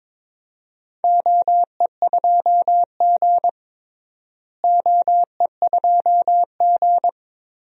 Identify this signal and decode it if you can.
Morse code, heard as a keyed tone, about 22 wpm, 700 Hz. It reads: OE2G OE2G
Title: DXCC-100 Call Signs - 4 Characters - Repeated 3x - 22wpm